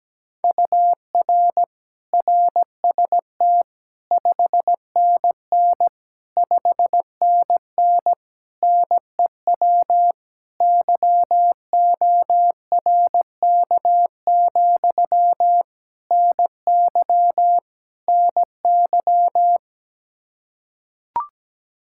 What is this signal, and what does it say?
Morse code, heard as a keyed tone, about 17 wpm, 700 Hz. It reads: UR RST 5NN 5NN NEW YORK, NY NY
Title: Practice QSO - Exchange 1 - Beginning - 17wpm